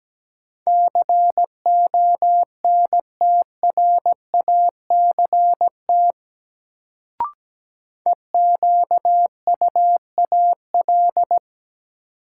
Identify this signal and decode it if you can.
Morse code, heard as a keyed tone, about 17 wpm, 700 Hz. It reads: CONTRACT EQUAL
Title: Sentences from Top 1400 Words - New Word Drill - 17wpm